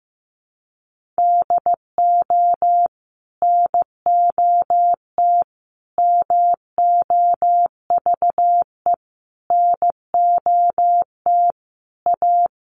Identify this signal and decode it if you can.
Morse code, heard as a keyed tone, about 15 wpm, 700 Hz. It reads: DO NOT MOVE NOT A
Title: Sentences from Top 1300 Words 15wpm